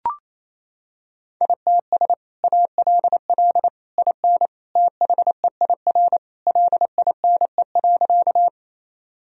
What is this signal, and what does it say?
Morse code, heard as a keyed tone, about 28 wpm, 700 Hz. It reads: ITS ALL IN THEIR LINE.
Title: Sentences from Top 1000 Words - Review Part 1 - 28wpm